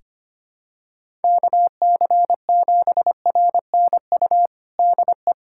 Morse code, heard as a keyed tone, about 25 wpm, 700 Hz. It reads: KC7RNU DE
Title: USA Callsign DE Callsign - 6 Characters - 25wpm